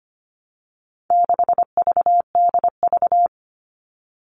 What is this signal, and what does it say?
Morse code, heard as a keyed tone, about 25 wpm, 700 Hz. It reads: 6VDV
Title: All Cut Numbers - 4-Digit 25wpm